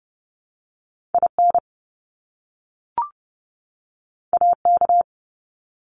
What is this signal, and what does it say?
Morse code, heard as a keyed tone, about 30 wpm, 700 Hz. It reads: IN AK
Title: ARRL Sections 30wpm